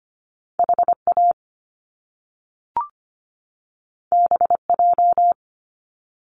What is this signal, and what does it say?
Morse code, heard as a keyed tone, about 25 wpm, 700 Hz. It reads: HA BJ
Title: Top Two Letter Combinations in Words 25wpm